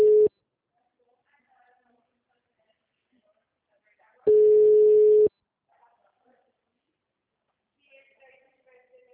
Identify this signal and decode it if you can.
Ringback tone (Europe, 425 Hz), with speech.